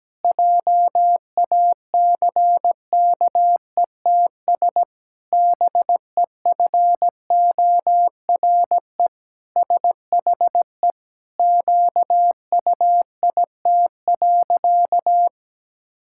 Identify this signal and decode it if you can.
Morse code, heard as a keyed tone, about 17 wpm, 700 Hz. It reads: JACKETS BEFORE SHE QUIT.
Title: Pangram Challenge v2 - 17wpm